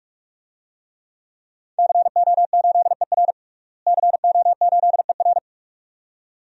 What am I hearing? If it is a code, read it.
Morse code, heard as a keyed tone, about 43 wpm, 700 Hz. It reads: CO8ER CO8ER
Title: DXCC-100 Call Signs - 5 Characters - Repeated 2x - 45wpm